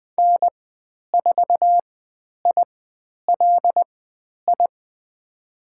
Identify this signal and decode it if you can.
Morse code, heard as a keyed tone, about 20 wpm, spaced as for 10 wpm, 700 Hz. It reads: N4ILI
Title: 102 Lesson 17g - Letter u - Call Signs 20wpm